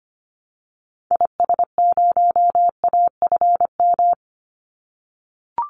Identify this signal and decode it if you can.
Morse code, heard as a keyed tone, about 25 wpm, 700 Hz. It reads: IS0AFM
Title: Call Signs - 25wpm